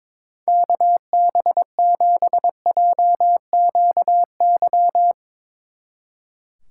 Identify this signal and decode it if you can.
Morse code, heard as a keyed tone, about 24 wpm, 700 Hz. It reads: KB7JQY
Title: Speed Racing - USA 2x3 Call Signs 22wpm